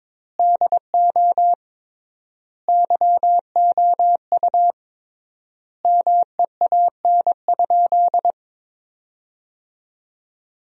Morse code, heard as a keyed tone, about 22 wpm, 700 Hz. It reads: DO YOU MEAN?
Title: 3x Word-Spacing - Sentences from Top 500 Words - 22wpm